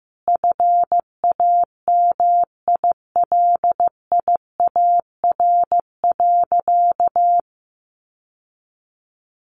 Morse code, heard as a keyed tone, about 15 wpm, 700 Hz. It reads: FAMILIAR.
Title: Sentences from Top 1000 Words 15wpm